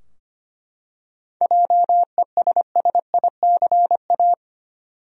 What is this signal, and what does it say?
Morse code, heard as a keyed tone, about 25 wpm, 700 Hz. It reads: JESSICA
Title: USA Common Names - 25wpm